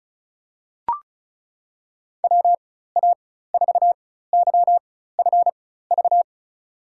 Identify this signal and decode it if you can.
Morse code, heard as a keyed tone, about 34 wpm, spaced as for 16 wpm, 700 Hz. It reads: WA4YFV
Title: USA Call Signs - 6 Characters - 4x Character Spacing - 35wpm